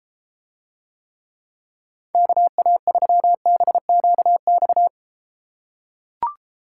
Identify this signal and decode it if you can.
Morse code, heard as a keyed tone, about 33 wpm, 700 Hz. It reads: KA3BQX